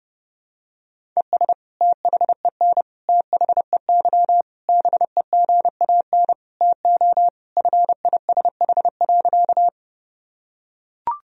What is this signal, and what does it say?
Morse code, heard as a keyed tone, about 30 wpm, 700 Hz. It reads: ES THEN THEY BEGAN TO FISH.
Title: Sentences from Top 300 Words and Abbreviations 30wpm